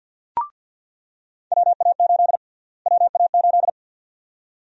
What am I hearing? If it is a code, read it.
Morse code, heard as a keyed tone, about 48 wpm, 700 Hz. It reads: WA8 WA8
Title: USA Call Sign Prefixes - Repeated 2x - 50wpm